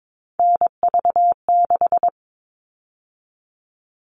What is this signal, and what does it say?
Morse code, heard as a keyed tone, about 22 wpm, 700 Hz. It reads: NV6
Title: All Cut Numbers - 3-Digit 22wpm